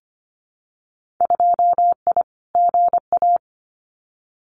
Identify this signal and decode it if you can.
Morse code, heard as a keyed tone, about 25 wpm, 700 Hz. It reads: 2I GA
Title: Winter Field Day Exchanges 25wpm